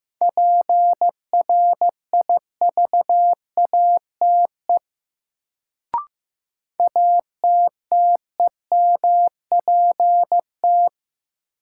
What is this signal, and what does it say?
Morse code, heard as a keyed tone, about 15 wpm, 700 Hz. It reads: PRIVATE ATTEMPT
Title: Sentences from Top 1000 Words - New Word Drill - 15wpm